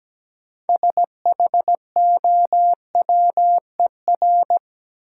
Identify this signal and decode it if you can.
Morse code, heard as a keyed tone, about 17 wpm, 700 Hz. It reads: SHOWER
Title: Sentences from Top 1600 Words - New Word Drill Alt - 17wpm